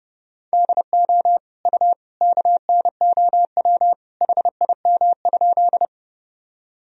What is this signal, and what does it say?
Morse code, heard as a keyed tone, about 30 wpm, 700 Hz. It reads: DO U KNOW HIM?
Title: Sentences from Top 100 Words and Abbreviations 30wpm